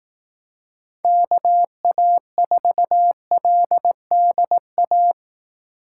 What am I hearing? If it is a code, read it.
Morse code, heard as a keyed tone, about 18 wpm, 700 Hz. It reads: KA4LDA
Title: Speed Racing - USA 2x3 Call Signs 15wpm